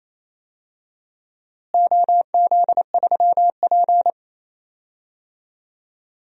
Morse code, heard as a keyed tone, about 28 wpm, 700 Hz. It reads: OZ3P
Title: DXCC-100 Call Signs - 4 Characters - Repeated 3x - 28wpm